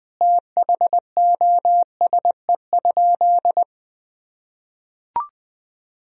Morse code, heard as a keyed tone, about 19 wpm, 700 Hz. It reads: THOSE?
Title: Sentences from Top 1000 Words - Review Part 1 - 20wpm